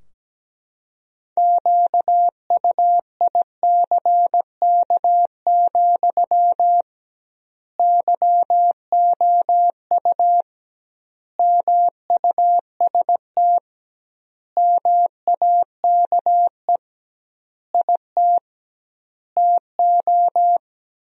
Morse code, heard as a keyed tone, about 17 wpm, 700 Hz. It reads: QUICK, YOU MUST MAKE IT TO
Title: 2x Word-Spacing - Sentences from Top 500 Words - 17wpm